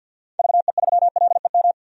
Morse code, heard as a keyed tone, about 46 wpm, 700 Hz. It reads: VE3LEM
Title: DXCC-100 Call Signs - 6 Characters - 50wpm